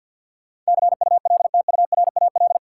Morse code, heard as a keyed tone, about 48 wpm, 700 Hz. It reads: CULTURAL
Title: Sentences from Top 1500 Words - New Word Drill - 50wpm